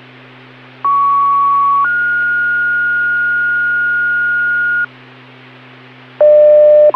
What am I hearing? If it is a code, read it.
Two-tone page, heard as a one-second tone, then a three-second tone, about 1120 then 1430 Hz, with noise. It is heard from 0.8 to 4.8 s.